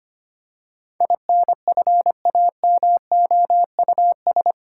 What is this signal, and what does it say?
Morse code, heard as a keyed tone, about 25 wpm, 690 Hz. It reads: INFAMOUS
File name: 8 Letter Words - First 3 Letters Given - 25wpm